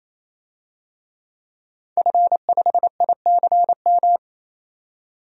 Morse code, heard as a keyed tone, about 28 wpm, 700 Hz. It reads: F5ICM
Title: DXCC-100 Call Signs - 5 Characters - 28wpm